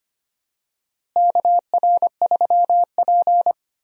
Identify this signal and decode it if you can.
Morse code, heard as a keyed tone, about 25 wpm, 700 Hz. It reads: KR3P
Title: USA Call Signs - 4 Characters - Repeated 2x - 25wpm